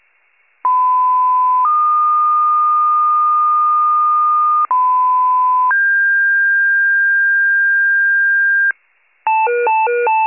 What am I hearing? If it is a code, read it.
Two-tone page, heard as a one-second tone, then a three-second tone, about 1010 then 1220 Hz, about 1010 then 1600 Hz, with noise. It is heard from 0.6 to 8.7 s.
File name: wcalert